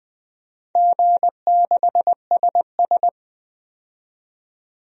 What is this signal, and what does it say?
Morse code, heard as a keyed tone, about 21 wpm, 700 Hz. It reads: G6SS
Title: DXCC-100 Call Signs - 4 Characters - 20wpm